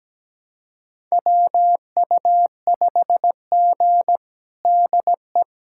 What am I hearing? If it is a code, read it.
Morse code, heard as a keyed tone, about 17 wpm, 700 Hz. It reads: WU5G DE
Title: USA Callsign DE Callsign - 4 Characters - 17wpm